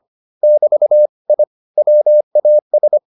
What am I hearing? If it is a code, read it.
Morse code, heard as a keyed tone, about 25 wpm, 600 Hz. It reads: <BT> I WAS